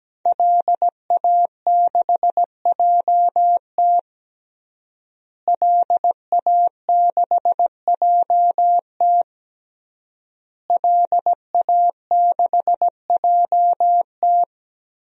Morse code, heard as a keyed tone, about 17 wpm, 700 Hz. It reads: LA6JT LA6JT LA6JT
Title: DXCC-100 Call Signs - 5 Characters - Repeated 3x - 17wpm